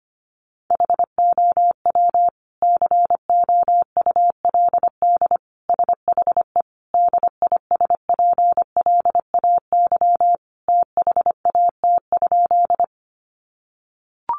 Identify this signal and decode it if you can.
Morse code, heard as a keyed tone, about 25 wpm, 700 Hz. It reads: HOW COULD SHE DISPLAY THAT?
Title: Sentences from Top 1200 Words 25wpm